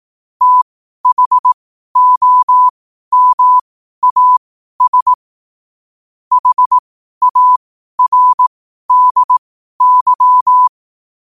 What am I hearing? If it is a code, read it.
Morse code, heard as a keyed tone, about 18 wpm, spaced as for 12 wpm, 1000 Hz. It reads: THOMAS HARDY
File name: qotdfiles / 12wpm